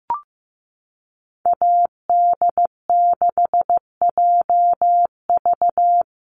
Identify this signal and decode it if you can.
Morse code, heard as a keyed tone, about 14 wpm, 700 Hz. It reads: AD6JV
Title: Call Signs Repeated 3x - 15wpm